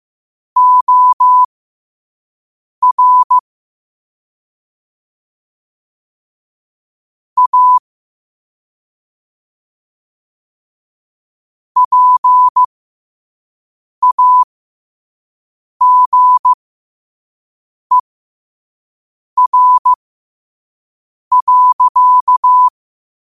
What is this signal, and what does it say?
Morse code, heard as a keyed tone, about 15 wpm, spaced as for 5 wpm, 1000 Hz. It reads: OR A PAGER.
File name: qotdfiles / 5wpm